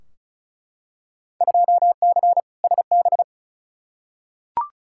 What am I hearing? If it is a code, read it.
Morse code, heard as a keyed tone, about 36 wpm, 700 Hz. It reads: 2C SB
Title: Field Day Exchanges Encore - 35wpm